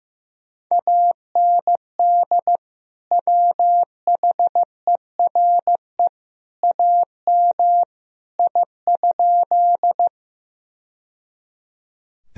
Morse code, heard as a keyed tone, about 15 wpm, 700 Hz. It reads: AND WHERE AM I?
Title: Sentences from Top 1000 Words - Review Part 2 - 15wpm